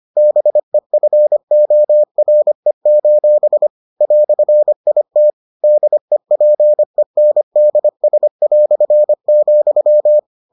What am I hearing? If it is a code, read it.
Morse code, heard as a keyed tone, about 25 wpm, 600 Hz. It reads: BEFORE: "IT DEPENDS",